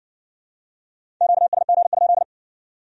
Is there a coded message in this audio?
Yes: Morse code from 1.2 to 2.2 s, heard as a keyed tone, about 56 wpm, 700 Hz. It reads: QSG?